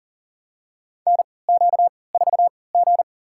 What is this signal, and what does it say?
Morse code, heard as a keyed tone, about 40 wpm, spaced as for 23 wpm, 700 Hz. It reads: NQ4G
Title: USA Call Signs - 4 Characters - 3x Character Spacing - 40wpm